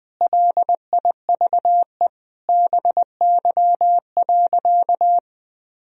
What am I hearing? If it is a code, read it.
Morse code, heard as a keyed tone, about 20 wpm, 700 Hz. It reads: LIVE BY.
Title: Sentences from Top 1000 Words - Review Part 3 - 20wpm